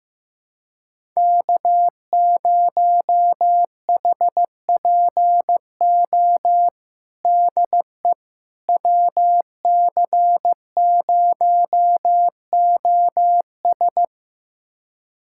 Morse code, heard as a keyed tone, about 15 wpm, 700 Hz. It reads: K0HPO DE WC0OS
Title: USA Callsign DE Callsign - 5 Characters - 15wpm